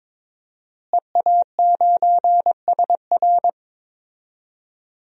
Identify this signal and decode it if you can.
Morse code, heard as a keyed tone, about 21 wpm, 700 Hz. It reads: EA9SR
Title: DXCC-100 Call Signs - 5 Characters - Repeated 2x - 22wpm